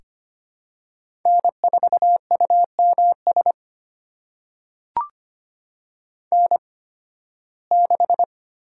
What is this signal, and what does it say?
Morse code, heard as a keyed tone, about 25 wpm, spaced as for 16 wpm, 700 Hz. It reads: N4UMS N6
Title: USA Call Signs - 8x Character Spacing - 25wpm